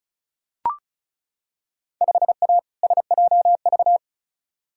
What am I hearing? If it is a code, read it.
Morse code, heard as a keyed tone, about 36 wpm, 700 Hz. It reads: 5A SJV
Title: Field Day Exchanges - 35wpm